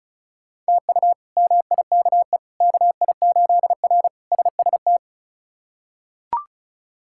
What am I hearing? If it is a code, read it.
Morse code, heard as a keyed tone, about 34 wpm, 700 Hz. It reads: TU MIKE KI8R SST
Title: K1USN SST QSO Practice - 35wpm